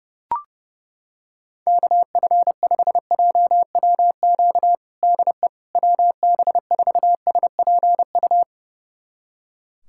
Morse code, heard as a keyed tone, about 30 wpm, 700 Hz. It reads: KF5JWQ DE WB4SPU
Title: USA Callsign DE Callsign - 6 Characters - 30wpm